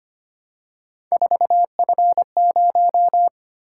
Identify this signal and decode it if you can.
Morse code, heard as a keyed tone, about 25 wpm, 700 Hz. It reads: 4F0
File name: Three Letter-Number Combinations - 25wpm